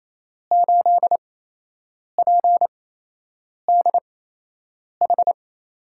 Morse code, heard as a keyed tone, about 27 wpm, spaced as for 8 wpm, 700 Hz. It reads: 8PDH
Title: USA Call Signs - 8x Character Spacing - 28wpm